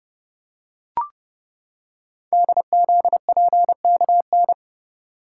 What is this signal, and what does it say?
Morse code, heard as a keyed tone, about 29 wpm, 700 Hz. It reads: DZPKN
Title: Code Groups - 5 Letters - 30wpm